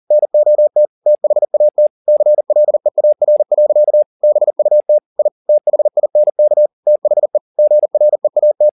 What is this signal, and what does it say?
Morse code, heard as a keyed tone, about 40 wpm, 600 Hz. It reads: NOT THAT CLEAR. BUT I THINK THE GREAT
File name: the old man and the sea 2 40wpm